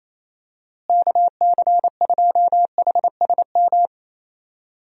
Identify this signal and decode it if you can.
Morse code, heard as a keyed tone, about 28 wpm, 700 Hz. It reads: KC2HSM
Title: USA Call Signs - 6 Characters - Repeated 2x - 28wpm